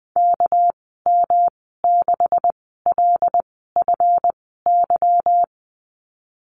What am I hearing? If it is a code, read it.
Morse code, heard as a keyed tone, about 20 wpm, spaced as for 15 wpm, 700 Hz. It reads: KM6LFY
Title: 238 Lesson 38g - Letter q - Call Signs 20wpm